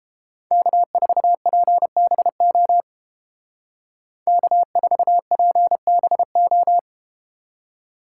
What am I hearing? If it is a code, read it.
Morse code, heard as a keyed tone, about 32 wpm, 700 Hz. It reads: K4PBO K4PBO